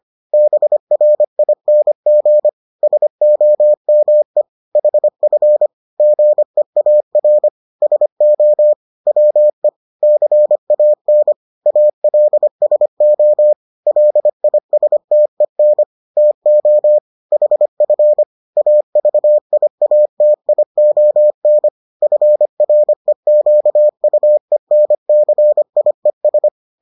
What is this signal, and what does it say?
Morse code, heard as a keyed tone, about 25 wpm, 600 Hz. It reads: BRING SOME HF GEAR SO WE CAN ALSO LISTEN TO HF AVIATION FREQUENCIES